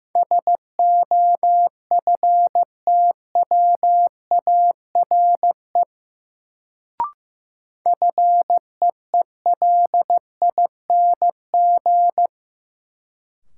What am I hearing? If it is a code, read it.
Morse code, heard as a keyed tone, about 15 wpm, 700 Hz. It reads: SOFTWARE FEELING
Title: Sentences from Top 1000 Words - Review Part 3 - New Word Drill - 15wpm